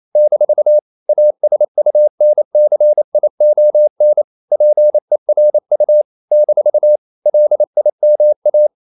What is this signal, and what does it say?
Morse code, heard as a keyed tone, about 28 wpm, 600 Hz. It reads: - ASUNCION PERU - LIMA
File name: CW / Practice / NationalCapitals28